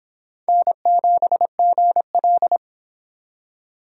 Morse code, heard as a keyed tone, about 26 wpm, 700 Hz. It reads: N7GL